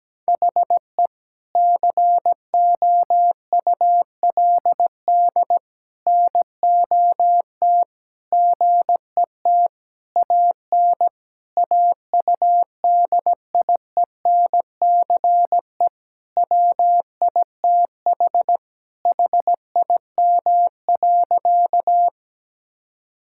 Morse code, heard as a keyed tone, about 17 wpm, 700 Hz. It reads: HE COULD NOT GET AN AUDIENCE WITH HIM.
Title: Sentences from Top 1000 Words - Review Part 5 - 17wpm